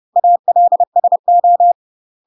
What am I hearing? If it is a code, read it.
Morse code, heard as a keyed tone, about 30 wpm, 700 Hz. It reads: ALSO
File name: UK MASTER QSOs TRAINING / 30WPM / UK MASTER QSOs 0005